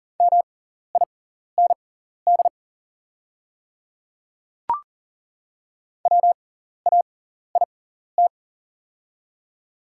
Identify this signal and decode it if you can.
Morse code, heard as a keyed tone, about 38 wpm, spaced as for 14 wpm, 700 Hz. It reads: MIND WAIT
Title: 6x Character Spacing - Top 500 Words - 40wpm